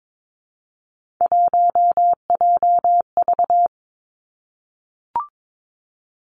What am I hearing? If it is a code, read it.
Morse code, heard as a keyed tone, about 22 wpm, 700 Hz. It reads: 1JV
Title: Three Challenging Letter-Number Combinations 22wpm